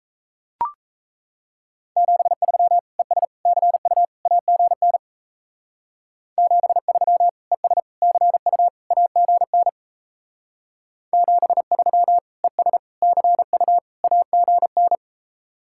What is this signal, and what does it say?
Morse code, heard as a keyed tone, about 33 wpm, 700 Hz. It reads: 73 ES CU AGN 73 ES CU AGN 73 ES CU AGN
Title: Speed-Racing Common QSO Phrases - 28wpm